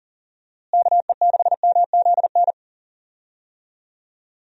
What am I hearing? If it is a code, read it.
Morse code, heard as a keyed tone, about 40 wpm, 700 Hz. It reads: KE6MZN
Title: USA Call Signs - 6 Characters - Repeated 3x - 40wpm